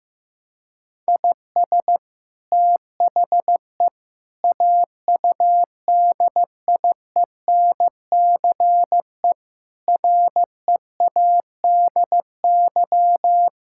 Morse code, heard as a keyed tone, about 15 wpm, 700 Hz. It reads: IS THE AUDIENCE READY
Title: Sentences from Top 1000 Words - Review Part 5 - 15wpm